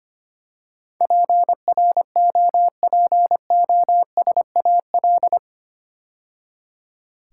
Morse code, heard as a keyed tone, about 25 wpm, 700 Hz. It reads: PROPOSAL
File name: 8 Letter Words - First 3 Letters Given - 25wpm